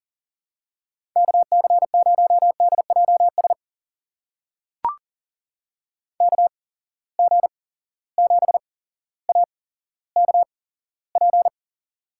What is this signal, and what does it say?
Morse code, heard as a keyed tone, about 40 wpm, spaced as for 17 wpm, 700 Hz. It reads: KC0DJS KG7AKP
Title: USA Call Signs - 6 Characters - 8x Character Spacing - 40wpm